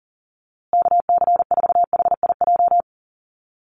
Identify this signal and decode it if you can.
Morse code, heard as a keyed tone, about 40 wpm, 700 Hz. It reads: KC4HIJ
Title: USA Call Signs Repeated 2x 40wpm